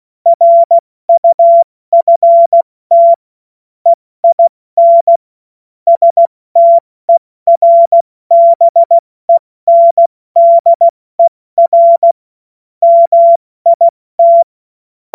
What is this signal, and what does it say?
Morse code, heard as a keyed tone, about 16 wpm, 680 Hz. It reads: RUFT EIN STERBENDER MIT
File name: Morsekurs / Witze / Witz 22. Juni 2021, 70 F80